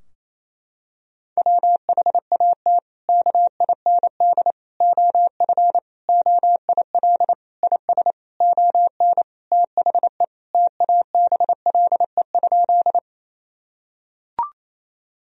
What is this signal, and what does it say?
Morse code, heard as a keyed tone, about 28 wpm, 700 Hz. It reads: WHAT KIND OF OIL IS ON THE TABLE?
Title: Sentences from Top 1000 Words - Review Part 4 - 28wpm